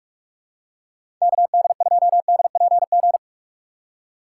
Morse code, heard as a keyed tone, about 41 wpm, 700 Hz. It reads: KD2DPG
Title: USA Call Signs - 6 Characters - 4x Character Spacing - 45wpm